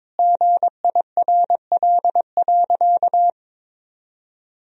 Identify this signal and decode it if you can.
Morse code, heard as a keyed tone, about 22 wpm, 700 Hz. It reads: GIRL.
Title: Sentences from Top 400 Words and Abbreviations 22wpm